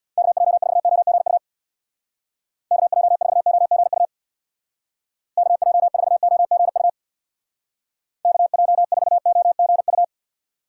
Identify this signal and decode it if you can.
Morse code, heard as a keyed tone, about 63 wpm, 700 Hz. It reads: KJ4OGU KJ4OGU KJ4OGU KJ4OGU